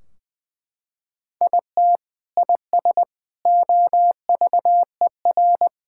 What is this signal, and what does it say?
Morse code, heard as a keyed tone, about 21 wpm, 700 Hz. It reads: IT IS OVER
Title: Sentences from Top 1600 Words 20wpm